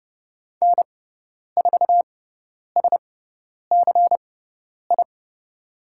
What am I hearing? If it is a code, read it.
Morse code, heard as a keyed tone, about 30 wpm, spaced as for 10 wpm, 700 Hz. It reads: N4SCI
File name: Lesson 17g - Letter u - Call Signs 30wpm